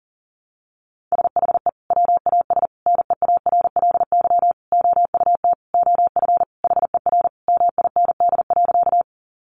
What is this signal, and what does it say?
Morse code, heard as a keyed tone, about 40 wpm, 700 Hz. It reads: SHE WAS NEARLY OUT OF HER MIND.
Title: Sentences from Top 1300 Words 40wpm